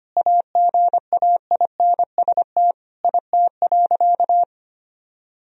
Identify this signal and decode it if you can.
Morse code, heard as a keyed tone, about 25 wpm, 700 Hz. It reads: AGAINST IT.
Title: Sentences from Top 1000 Words - Review Part 1 - 25wpm